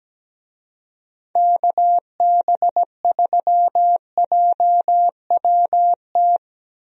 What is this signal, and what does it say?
Morse code, heard as a keyed tone, about 17 wpm, 700 Hz. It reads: KB3JWT
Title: USA Call Signs - 6 Characters - 17wpm